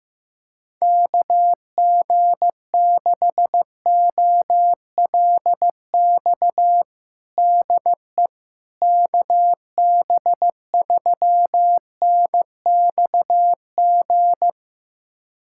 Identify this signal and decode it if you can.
Morse code, heard as a keyed tone, about 15 wpm, 700 Hz. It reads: KG6OLX DE KB3NXG
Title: USA Callsign DE Callsign - 6 Characters - 15wpm